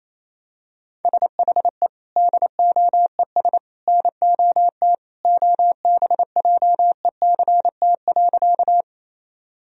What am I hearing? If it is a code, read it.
Morse code, heard as a keyed tone, about 27 wpm, 700 Hz. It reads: SHE DOES NOT OBJECT.
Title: Sentences from Top 1000 Words - Review Part 3 - 28wpm